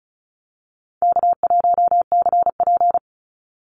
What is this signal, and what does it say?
Morse code, heard as a keyed tone, about 34 wpm, 700 Hz. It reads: K1CP
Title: Call Signs Repeated 2x - 35wpm